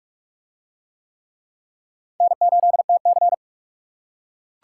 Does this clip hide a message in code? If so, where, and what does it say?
Yes: Morse code from 2.2 to 3.3 s, heard as a keyed tone, about 40 wpm, 700 Hz. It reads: N8TC